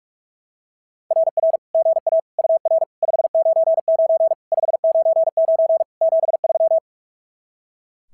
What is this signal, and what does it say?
Morse code, heard as a keyed tone, about 42 wpm, 660 Hz. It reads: RR GA UR 599 599 73
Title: SOTA QSO Practice 45wpm